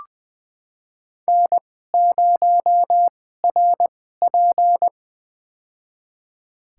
Morse code, heard as a keyed tone, about 21 wpm, spaced as for 15 wpm, 700 Hz. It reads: N0RP
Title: USA Call Signs - 2x Character Spacing - 20wpm